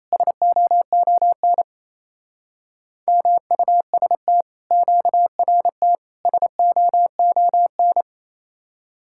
Morse code, heard as a keyed tone, about 26 wpm, 700 Hz. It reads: SOON MUST QRT SOON
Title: Speed-Racing Common QSO Phrases Encore - 28wpm